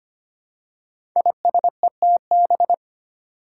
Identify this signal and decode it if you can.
Morse code, heard as a keyed tone, about 25 wpm, 700 Hz. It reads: ISETB
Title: Five Challenging Letter-Number Combinations 25wpm